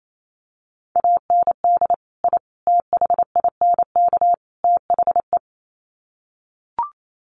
Morse code, heard as a keyed tone, about 28 wpm, 700 Hz. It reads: AND I THINK THE
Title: Sets of 4 Words 28wpm